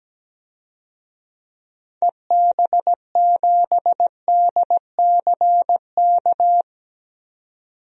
Morse code, heard as a keyed tone, about 17 wpm, 700 Hz. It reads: EB7DCK
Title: DXCC-100 Call Signs - 6 Characters - 17wpm